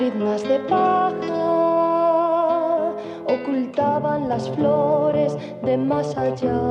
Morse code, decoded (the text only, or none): none